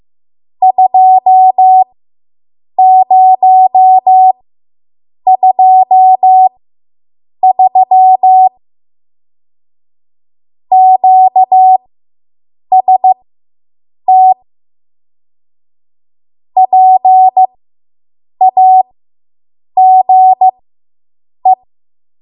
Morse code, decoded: 2023 QST PAGE